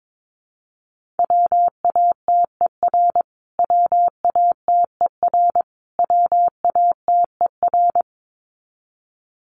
Morse code, heard as a keyed tone, about 22 wpm, 700 Hz. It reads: WATER WATER WATER
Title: Top 500 Words Repeated 22wpm